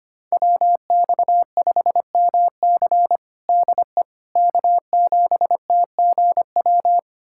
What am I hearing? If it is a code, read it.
Morse code, heard as a keyed tone, about 25 wpm, 700 Hz. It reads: WX5MC DE K7TGW
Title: USA Callsign DE Callsign - 5 Characters - 25wpm